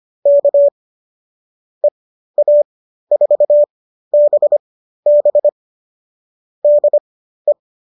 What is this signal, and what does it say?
Morse code, heard as a keyed tone, about 25 wpm, spaced as for 13 wpm, 590 Hz. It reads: K EA4BB DE